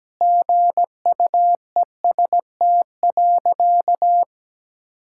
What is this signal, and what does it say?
Morse code, heard as a keyed tone, about 17 wpm, 700 Hz. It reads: GUEST.
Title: Sentences from Top 1600 Words 17wpm